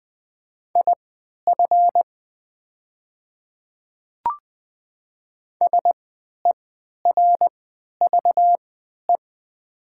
Morse code, heard as a keyed tone, about 20 wpm, spaced as for 11 wpm, 700 Hz. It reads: IF SERVE
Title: 3x Character Spacing - Top 500 Words - 20wpm